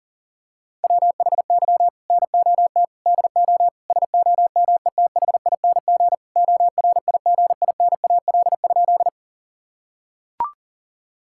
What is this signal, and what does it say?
Morse code, heard as a keyed tone, about 40 wpm, 700 Hz. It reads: WHY NOT DO SOMETHING ORIGINAL?
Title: Sentences from Top 1000 Words - Review Part 6 - 40wpm